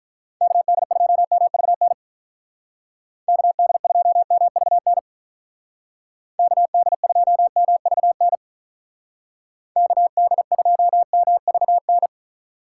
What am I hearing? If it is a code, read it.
Morse code, heard as a keyed tone, about 43 wpm, 700 Hz. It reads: KD2MVN KD2MVN KD2MVN KD2MVN
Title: Speed Racing - USA 2x3 Call Signs 35wpm